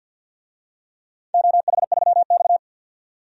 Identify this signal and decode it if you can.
Morse code, heard as a keyed tone, about 48 wpm, 700 Hz. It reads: OH3X
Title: DXCC-100 Call Signs - 4 Characters - 50wpm